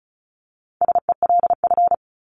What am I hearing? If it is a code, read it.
Morse code, heard as a keyed tone, about 34 wpm, 700 Hz. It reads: SELF
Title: Top 500 Words 35wpm